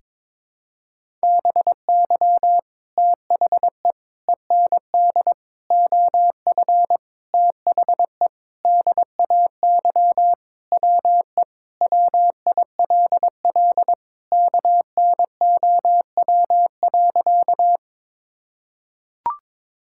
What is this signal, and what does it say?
Morse code, heard as a keyed tone, about 22 wpm, 700 Hz. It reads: BY THE END OF THE DAY WE WILL KNOW.
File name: practice / 22wpm / Sentences with 4-Word Phrases - 22wpm